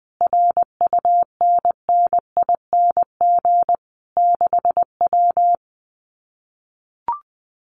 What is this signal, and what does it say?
Morse code, heard as a keyed tone, about 20 wpm, 700 Hz. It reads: RUNNING 6W
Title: Common QSO Phrases Encore 20wpm